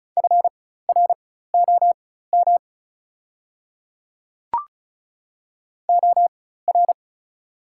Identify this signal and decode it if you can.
Morse code, heard as a keyed tone, about 34 wpm, spaced as for 16 wpm, 690 Hz. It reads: FROM OR
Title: 4x Character Spacing - Top 200 Words - 35wpm